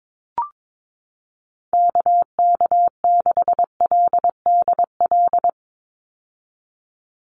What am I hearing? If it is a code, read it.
Morse code, heard as a keyed tone, about 22 wpm, 700 Hz. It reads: KK6LDL
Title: USA Call Signs 22wpm